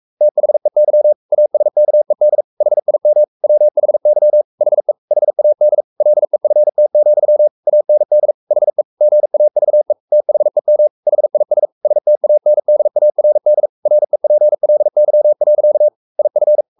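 Morse code, heard as a keyed tone, about 44 wpm, 600 Hz. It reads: THEY ASKED HIM WHY HE HAD LEFT, AND HE GAVE THEM HIS STANDARD REPLY. IF
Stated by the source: Morse code at 43 wpm sending Ken Follett's novel Never in English, from CW med Gnister.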